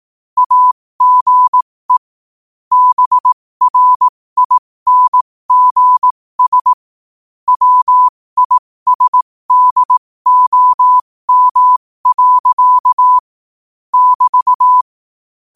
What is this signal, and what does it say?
Morse code, heard as a keyed tone, about 18 wpm, spaced as for 15 wpm, 1000 Hz. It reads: AGE BRINGS WISDOM. <BT>
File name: qotdfiles / 15wpm